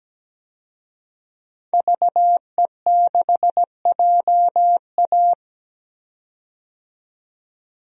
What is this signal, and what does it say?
Morse code, heard as a keyed tone, about 17 wpm, 700 Hz. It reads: VE6JA
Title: DXCC-100 Call Signs - 5 Characters - 17wpm